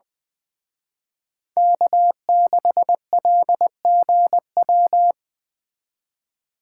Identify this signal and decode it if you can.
Morse code, heard as a keyed tone, about 23 wpm, 700 Hz. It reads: K6LGW
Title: Speed Racing - USA Call Signs 20wpm